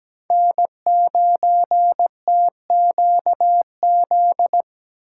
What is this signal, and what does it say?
Morse code, heard as a keyed tone, about 17 wpm, 700 Hz. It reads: N9TQZ
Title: USA Call Signs - 5 Characters - 2x Character Spacing - 17wpm